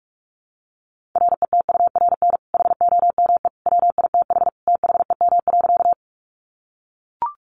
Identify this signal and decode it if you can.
Morse code, heard as a keyed tone, about 44 wpm, 700 Hz. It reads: RETURN HOME WITH THEM.
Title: Sentences from Top 800 Words 45wpm